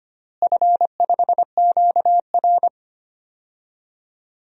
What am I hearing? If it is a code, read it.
Morse code, heard as a keyed tone, about 25 wpm, 700 Hz. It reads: F5QR
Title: DXCC-100 Call Signs - 4 Characters - Repeated 2x - 25wpm